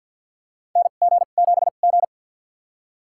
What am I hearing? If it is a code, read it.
Morse code, heard as a keyed tone, about 47 wpm, spaced as for 35 wpm, 690 Hz. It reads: NG7G
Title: USA Call Signs - 2x Character Spacing - 50wpm